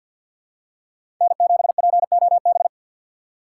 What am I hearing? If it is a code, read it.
Morse code, heard as a keyed tone, about 51 wpm, 700 Hz. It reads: N7POB
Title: USA Call Signs - 5 Characters - 6x Character Spacing - 50wpm